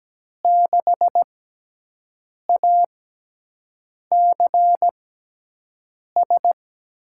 Morse code, heard as a keyed tone, about 17 wpm, spaced as for 6 wpm, 700 Hz. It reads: 6ACS
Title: USA Call Signs - 6x Character Spacing - 17wpm